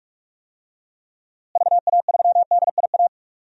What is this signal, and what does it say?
Morse code, heard as a keyed tone, about 46 wpm, 700 Hz. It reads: VA3DIA
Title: DXCC-100 Call Signs - 6 Characters - Repeated 2x - 45wpm